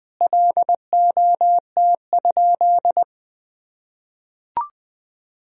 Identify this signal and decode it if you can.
Morse code, heard as a keyed tone, about 20 wpm, 700 Hz. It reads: LOT?
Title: Sentences from Top 1000 Words - Review Part 4 - 20wpm